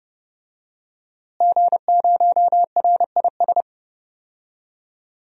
Morse code, heard as a keyed tone, about 30 wpm, 700 Hz. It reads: G0RIS
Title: DXCC-100 Call Signs - 5 Characters - Repeated 3x - 30wpm